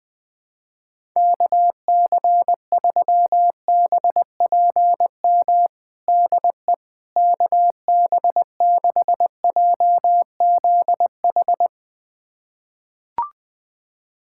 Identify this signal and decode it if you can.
Morse code, heard as a keyed tone, about 20 wpm, 700 Hz. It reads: KC3BPM DE KB6JZH
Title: USA Callsign DE Callsign - 6 Characters - 20wpm